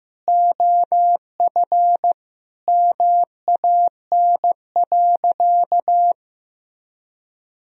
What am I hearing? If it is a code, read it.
Morse code, heard as a keyed tone, about 15 wpm, 700 Hz. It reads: OF MAN.
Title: Sentences from Top 1000 Words - Review Part 4 - 15wpm